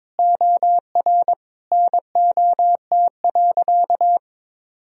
Morse code, heard as a keyed tone, about 21 wpm, 700 Hz. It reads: OR NOT.
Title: Sentences from Top 1000 Words - Review Part 3 - 22wpm